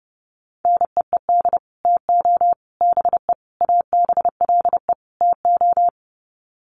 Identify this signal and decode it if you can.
Morse code, heard as a keyed tone, about 30 wpm, 700 Hz. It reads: NEED TO BE ABLE TO
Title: Sets of 5 Words Encore 30wpm